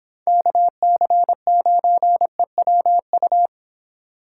Morse code, heard as a keyed tone, about 28 wpm, 700 Hz. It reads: KC9EWU